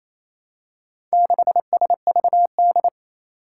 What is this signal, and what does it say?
Morse code, heard as a keyed tone, about 28 wpm, 700 Hz. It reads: 6SVD
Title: Four Challenging Letter-Number Combinations 28wpm